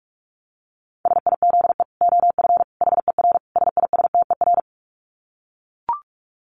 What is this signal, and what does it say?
Morse code, heard as a keyed tone, about 44 wpm, 700 Hz. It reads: SIZE OF HER SISTER
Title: Dit Challenge 45wpm